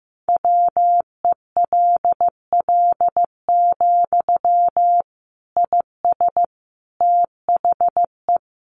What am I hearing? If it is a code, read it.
Morse code, heard as a keyed tone, about 15 wpm, 700 Hz. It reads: WELL, IS THE
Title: Sentences from Top 800 Words 15wpm